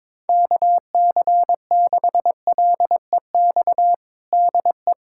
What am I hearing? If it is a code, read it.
Morse code, heard as a keyed tone, about 22 wpm, 700 Hz. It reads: KC6LEX DE
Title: USA Callsign DE Callsign 22wpm